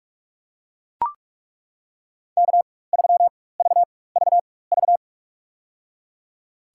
Morse code, heard as a keyed tone, about 42 wpm, spaced as for 20 wpm, 700 Hz. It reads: K3VVV
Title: USA Call Signs - 5 Characters - 4x Character Spacing - 45wpm